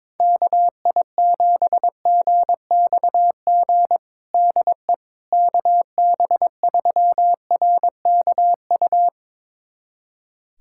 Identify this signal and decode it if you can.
Morse code, heard as a keyed tone, about 22 wpm, 700 Hz. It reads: KI7GXG DE KB3RKU
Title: USA Callsign DE Callsign 22wpm